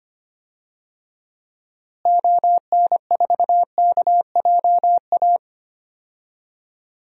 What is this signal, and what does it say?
Morse code, heard as a keyed tone, about 25 wpm, 700 Hz. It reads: ON4KJA
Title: DXCC-100 Call Signs - 6 Characters - Repeated 2x - 25wpm